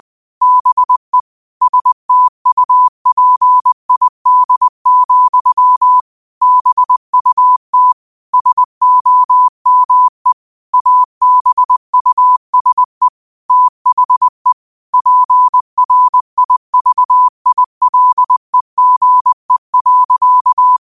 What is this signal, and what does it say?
Morse code, heard as a keyed tone, about 20 wpm, 1000 Hz. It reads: BE STUPID, BUT SOME ABUSE THE PRIVILEGE.